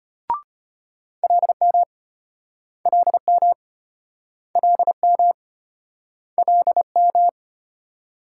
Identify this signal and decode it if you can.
Morse code, heard as a keyed tone, about 35 wpm, 700 Hz. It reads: LM LM LM LM